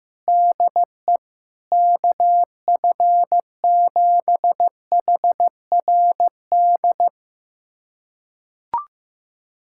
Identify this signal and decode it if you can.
Morse code, heard as a keyed tone, about 15 wpm, 700 Hz. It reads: DE KF7HRD
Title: USA Callsign DE Callsign - 6 Characters - 15wpm